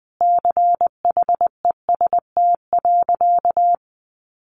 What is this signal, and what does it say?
Morse code, heard as a keyed tone, about 20 wpm, 700 Hz. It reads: CHEST.
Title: A Sound Of Thunder 20wpm